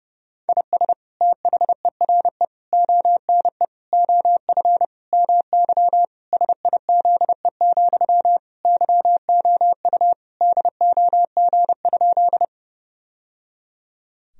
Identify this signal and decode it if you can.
Morse code, heard as a keyed tone, about 30 wpm, 700 Hz. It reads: IS THERE ONE OF MY SIZE, YOU DOG?
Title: Sentences from Top 1000 Words - Review Part 2 - 30wpm